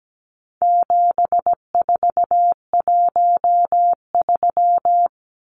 Morse code, heard as a keyed tone, about 17 wpm, 700 Hz. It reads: 7413
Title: Numbers - 4-Digit 17wpm